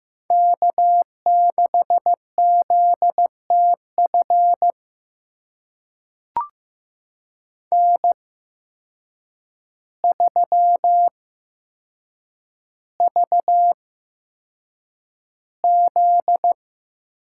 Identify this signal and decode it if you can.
Morse code, heard as a keyed tone, about 15 wpm, spaced as for 7 wpm, 700 Hz. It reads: K6ZTF N3VZ
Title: USA Call Signs - 5 Characters - 8x Character Spacing - 15wpm